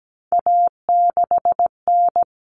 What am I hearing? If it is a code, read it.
Morse code, heard as a keyed tone, about 17 wpm, 700 Hz. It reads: A6N
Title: All Cut Numbers - 3-Digit 17wpm